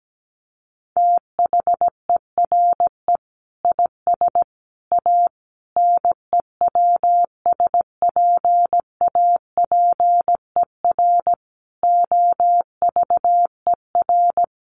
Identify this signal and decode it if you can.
Morse code, heard as a keyed tone, about 17 wpm, 700 Hz. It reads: THERE IS A NEWSPAPER OVER
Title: Sentences from Top 1100 Words 17wpm